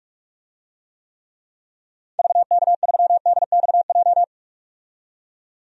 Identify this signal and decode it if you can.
Morse code, heard as a keyed tone, about 44 wpm, 700 Hz. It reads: VK3DXJ
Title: DXCC-100 Call Signs - 6 Characters - 45wpm